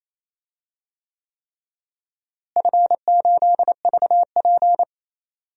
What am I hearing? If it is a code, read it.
Morse code, heard as a keyed tone, about 28 wpm, 700 Hz. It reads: F8VP
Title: DXCC-100 Call Signs - 4 Characters - Repeated 3x - 28wpm